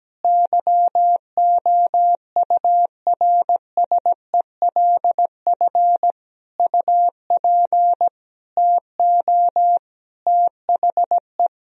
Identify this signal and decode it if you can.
Morse code, heard as a keyed tone, about 17 wpm, 700 Hz. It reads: YOURSELF UP TO THE
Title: Sentences from Top 1000 Words - Review Part 4 - 17wpm